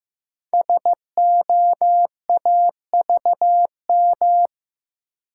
Morse code, heard as a keyed tone, about 15 wpm, 700 Hz. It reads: SOAVM
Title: Code Groups - 5 Letters - 15wpm